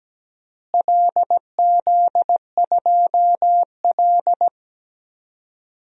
Morse code, heard as a keyed tone, about 17 wpm, 700 Hz. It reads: LZ2L
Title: DXCC-100 Call Signs - 4 Characters - Repeated 3x - 17wpm